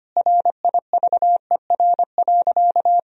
Morse code, heard as a keyed tone, about 25 wpm, 700 Hz. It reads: RIVER.
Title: Sentences from Top 1000 Words - Review Part 1 - 25wpm